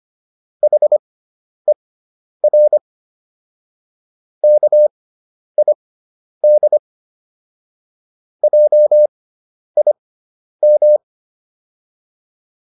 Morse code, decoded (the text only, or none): HER KID JIM